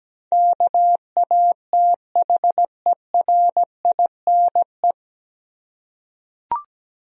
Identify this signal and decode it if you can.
Morse code, heard as a keyed tone, about 17 wpm, 700 Hz. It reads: KATHERINE
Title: USA Common Names - 17wpm